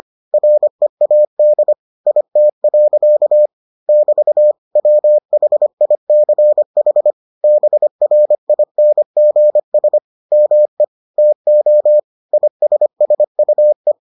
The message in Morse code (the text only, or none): READ IT. <BT> WHICH BRINGS ME TO ISSUE